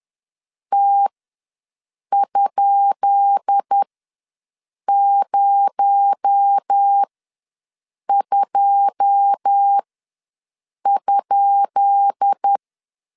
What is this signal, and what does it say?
Morse code, heard as a keyed tone, about 10 wpm, spaced as for 6 wpm, 800 Hz. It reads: T?02?
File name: LEZIONI / 10 SIGNS / 10B INT ws 10wpm 4134